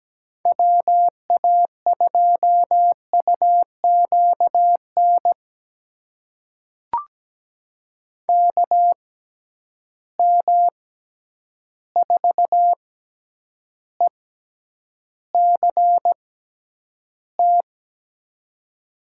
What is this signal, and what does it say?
Morse code, heard as a keyed tone, about 17 wpm, spaced as for 9 wpm, 700 Hz. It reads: WA2UQN KM4ECT
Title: USA Call Signs - 6 Characters - 6x Character Spacing - 17wpm